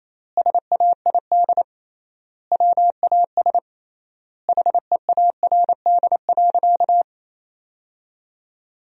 Morse code, heard as a keyed tone, about 28 wpm, 700 Hz. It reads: SAID WAS HEARD.
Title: 3x Word-Spacing - Sentences from Top 400 Words - 28wpm